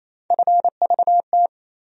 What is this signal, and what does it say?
Morse code, heard as a keyed tone, about 28 wpm, 700 Hz. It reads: FVT
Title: USA Call Sign Postfixes - 3 Letters 28wpm